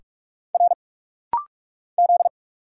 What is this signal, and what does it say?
Morse code, heard as a keyed tone, about 46 wpm, 700 Hz. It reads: R Z
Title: Single Letters - Rapid-Fire 45wpm